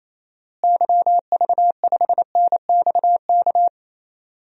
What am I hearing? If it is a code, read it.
Morse code, heard as a keyed tone, about 28 wpm, 700 Hz. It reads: YV5NXK
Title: DXCC-100 Call Signs - 6 Characters - Repeated 2x - 28wpm